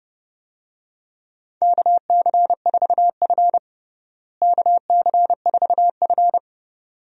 Morse code, heard as a keyed tone, about 30 wpm, 700 Hz. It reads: KC4F KC4F
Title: USA Call Signs - 4 Characters - Repeated 2x - 30wpm